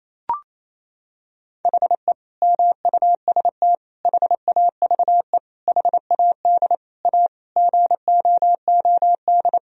Morse code, heard as a keyed tone, about 27 wpm, 700 Hz. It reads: HE MUST HAVE HAD A GOOD
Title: Sentences from Top 1000 Words - Review Part 1 - 28wpm